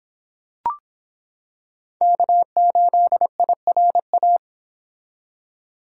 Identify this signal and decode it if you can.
Morse code, heard as a keyed tone, about 26 wpm, 700 Hz. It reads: K8IRA